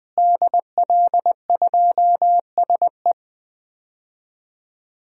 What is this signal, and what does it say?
Morse code, heard as a keyed tone, about 20 wpm, 700 Hz. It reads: DL2SE
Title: DXCC-100 Call Signs - 5 Characters - Repeated 2x - 20wpm